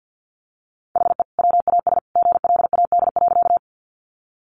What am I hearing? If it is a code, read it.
Morse code, heard as a keyed tone, about 50 wpm, 700 Hz. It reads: HE WAS GLAD.
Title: Sentences from Top 800 Words 50wpm